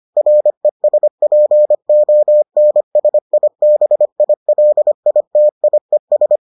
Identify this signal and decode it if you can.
Morse code, heard as a keyed tone, about 25 wpm, 600 Hz. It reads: RESPONSIBILITIES